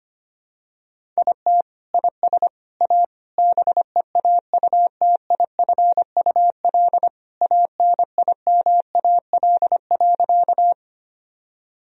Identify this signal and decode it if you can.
Morse code, heard as a keyed tone, about 25 wpm, 700 Hz. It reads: IT IS A BEAUTIFUL ANIMAL.
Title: Sentences from Top 1000 Words - Review Part 1 - 25wpm